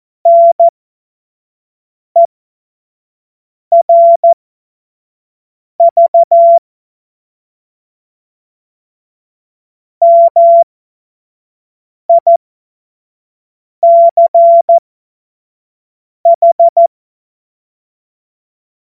Morse code, heard as a keyed tone, about 14 wpm, spaced as for 5 wpm, 680 Hz. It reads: NERV MICH
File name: Morsekurs / Witze / Witz 25. Mai 2021, 25 F70